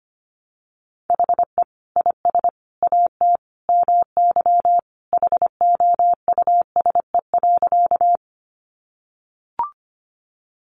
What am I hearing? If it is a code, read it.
Morse code, heard as a keyed tone, about 25 wpm, 700 Hz. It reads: HE IS AT MY HOUSE.
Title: Sentences from Top 200 Words 25wpm